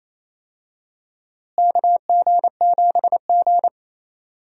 Morse code, heard as a keyed tone, about 28 wpm, 700 Hz. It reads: KG7G
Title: USA Call Signs - 4 Characters - Repeated 2x - 28wpm